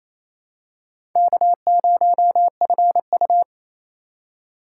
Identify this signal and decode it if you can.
Morse code, heard as a keyed tone, about 28 wpm, 700 Hz. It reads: K0FU
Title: USA Call Signs - 4 Characters - 3x Character Spacing - 28wpm